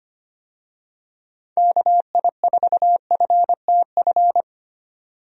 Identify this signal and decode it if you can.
Morse code, heard as a keyed tone, about 25 wpm, 700 Hz. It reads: KI4FTF
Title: USA Call Signs - 6 Characters - 25wpm